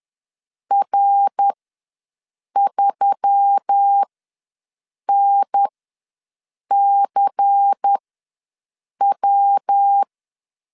Morse code, heard as a keyed tone, about 10 wpm, spaced as for 6 wpm, 800 Hz. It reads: R3NCW
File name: LEZIONI / 6 PKX0 / 6 PK ws 10wpm 3745